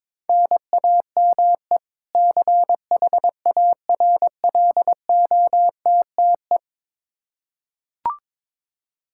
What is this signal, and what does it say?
Morse code, heard as a keyed tone, about 22 wpm, 700 Hz. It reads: NAME CHARLOTTE
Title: Practice QSO - Exchange 1 - Beginning - 22wpm